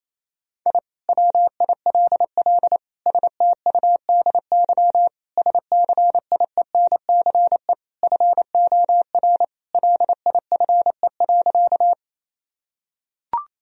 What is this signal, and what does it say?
Morse code, heard as a keyed tone, about 28 wpm, 700 Hz. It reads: I WILL STUDY SCIENCE FOR LIFE.
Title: Sentences from Top 1000 Words - Review Part 2 - 28wpm